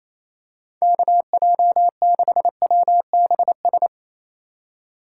Morse code, heard as a keyed tone, about 28 wpm, 700 Hz. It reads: KJ6WBS
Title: USA Call Signs - 4x Character Spacing - 28wpm